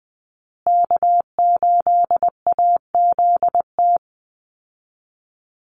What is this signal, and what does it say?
Morse code, heard as a keyed tone, about 20 wpm, 700 Hz. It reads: K8AZT
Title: Call Signs Repeated 3x - 20wpm